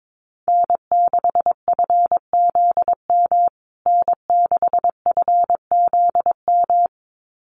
Morse code, heard as a keyed tone, about 22 wpm, 700 Hz. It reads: N6FZM N6FZM
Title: USA Call Signs Repeated 2x 22wpm